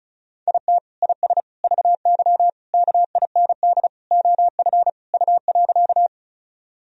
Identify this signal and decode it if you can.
Morse code, heard as a keyed tone, about 34 wpm, 700 Hz. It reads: IT IS VY KIND OF U.
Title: Sentences from Top 200 Words and Abbreviations 35wpm